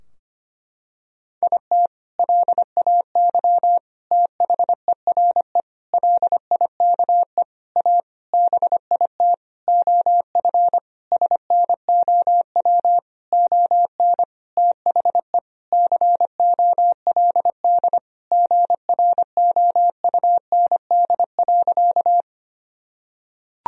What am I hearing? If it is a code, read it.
Morse code, heard as a keyed tone, about 25 wpm, 700 Hz. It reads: IT LAY THERE LIKE A BIT OF SNOW ON THE COLD GROUND.
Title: Sentences from Top 1000 Words - Review Part 3 - 25wpm